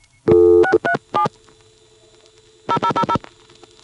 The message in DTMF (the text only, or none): 637 0000